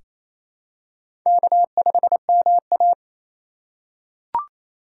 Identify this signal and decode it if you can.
Morse code, heard as a keyed tone, about 27 wpm, 700 Hz. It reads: K5MA
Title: K1USN SST QSO Practice - 28wpm